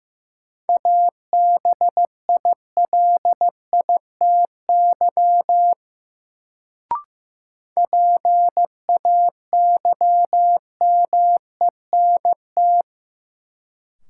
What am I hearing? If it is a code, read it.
Morse code, heard as a keyed tone, about 15 wpm, 700 Hz. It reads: ABILITY PAYMENT
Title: Sentences from Top 1200 Words - New Word Drill Alt - 15wpm